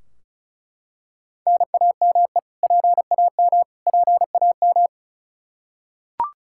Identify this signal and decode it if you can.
Morse code, heard as a keyed tone, about 36 wpm, 700 Hz. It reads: NAME PAM PAM
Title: Practice QSO - Exchange 1 - Beginning - 35wpm